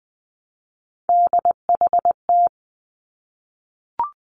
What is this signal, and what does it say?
Morse code, heard as a keyed tone, about 20 wpm, 700 Hz. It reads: DHT
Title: Three Challenging Letter-Number Combinations 20wpm